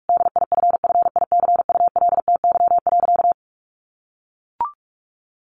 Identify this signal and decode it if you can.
Morse code, heard as a keyed tone, about 44 wpm, 700 Hz. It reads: DIFFICULTY.
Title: Sentences from Top 1500 Words 45wpm